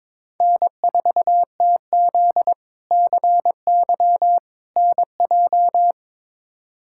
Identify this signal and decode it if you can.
Morse code, heard as a keyed tone, about 22 wpm, 700 Hz. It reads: N4TZ CY NJ
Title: K1USN SST QSO Practice - 22wpm